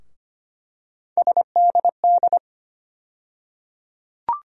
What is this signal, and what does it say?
Morse code, heard as a keyed tone, about 25 wpm, 700 Hz. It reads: SDD
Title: USA Call Sign Postfixes - 3 Letters 25wpm